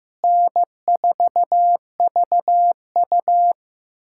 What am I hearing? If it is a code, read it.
Morse code, heard as a keyed tone, about 15 wpm, 700 Hz. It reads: N4VU
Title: USA Callsign DE Callsign - 4 Characters - 15wpm